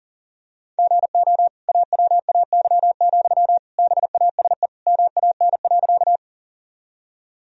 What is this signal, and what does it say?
Morse code, heard as a keyed tone, about 40 wpm, 700 Hz. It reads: GO AWAY, BASE MAN.
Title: Sentences from Top 1000 Words - Review Part 1 - 40wpm